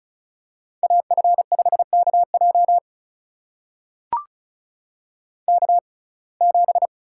Morse code, heard as a keyed tone, about 34 wpm, spaced as for 25 wpm, 700 Hz. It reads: AF5KJ K7
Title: USA Call Signs - 5 Characters - 6x Character Spacing - 35wpm